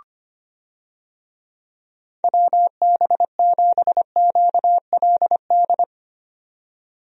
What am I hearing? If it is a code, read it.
Morse code, heard as a keyed tone, about 25 wpm, 700 Hz. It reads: WB7QLD
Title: USA Call Signs - 6 Characters - 25wpm